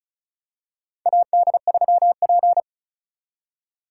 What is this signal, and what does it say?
Morse code, heard as a keyed tone, about 36 wpm, 700 Hz. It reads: AD3P
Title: USA Call Signs - 2x Character Spacing - 35wpm